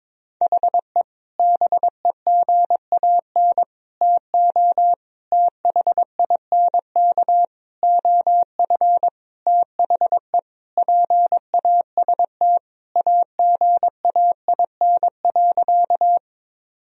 Morse code, heard as a keyed tone, about 22 wpm, 700 Hz. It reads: HE BEGAN TO THINK OF THE PAST AGAIN.
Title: Sentences from Top 1000 Words - Review Part 3 - 22wpm